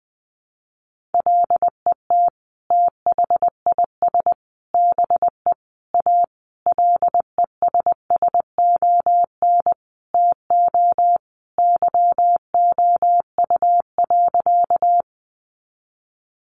Morse code, heard as a keyed tone, about 20 wpm, 700 Hz. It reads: LET THIS BE A LESSON TO YOU.
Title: Sentences from Top 1200 Words 20wpm